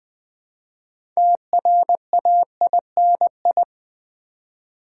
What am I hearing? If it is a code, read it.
Morse code, heard as a keyed tone, about 20 wpm, 690 Hz. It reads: TRAINI
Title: Build A Word - 3 Letters At A Time - 8-Letter Words - 20wpm